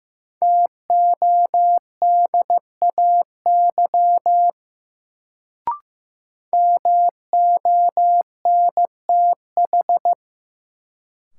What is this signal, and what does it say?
Morse code, heard as a keyed tone, about 15 wpm, 700 Hz. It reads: TODAY MONTH
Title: Sets of 5 Words Encore - New Word Drill - 15wpm